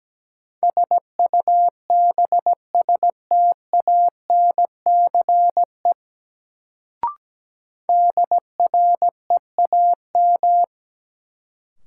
Morse code, heard as a keyed tone, about 17 wpm, 700 Hz. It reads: SUBSTANCE DREAM
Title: Sentences from Top 1500 Words - New Word Drill Alt - 17wpm